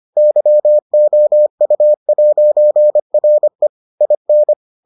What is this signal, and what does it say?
Morse code, heard as a keyed tone, about 25 wpm, 600 Hz. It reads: YOU'RE IN